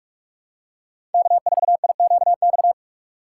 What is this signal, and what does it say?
Morse code, heard as a keyed tone, about 44 wpm, 700 Hz. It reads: K4IQX
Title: USA Call Signs - 5 Characters - 2x Character Spacing - 45wpm